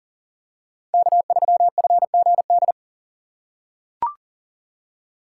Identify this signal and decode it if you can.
Morse code, heard as a keyed tone, about 38 wpm, 700 Hz. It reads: K3FGD
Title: USA Call Signs - 5 Characters - 4x Character Spacing - 40wpm